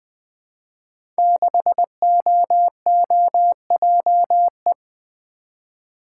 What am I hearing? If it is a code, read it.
Morse code, heard as a keyed tone, about 20 wpm, 700 Hz. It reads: 6OOJE
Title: Code Groups - 5 Letter:Numbers - 20wpm